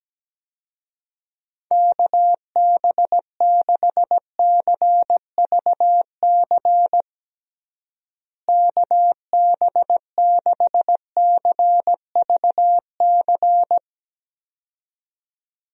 Morse code, heard as a keyed tone, about 17 wpm, 700 Hz. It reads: KB6CVC KB6CVC
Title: USA Call Signs - 6 Characters - Repeated 2x - 17wpm